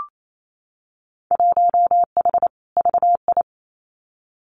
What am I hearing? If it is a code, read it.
Morse code, heard as a keyed tone, about 28 wpm, 700 Hz. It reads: 1H VI
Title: Winter Field Day Exchanges Encore 28wpm